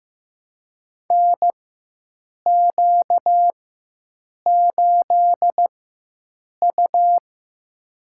Morse code, heard as a keyed tone, about 15 wpm, spaced as for 7 wpm, 700 Hz. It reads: NQ8U
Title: USA Call Signs - 4x Character Spacing - 15wpm